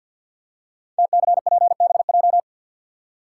Morse code, heard as a keyed tone, about 52 wpm, 700 Hz. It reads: TCPBJ
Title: Code Groups - 5 Letters - 50wpm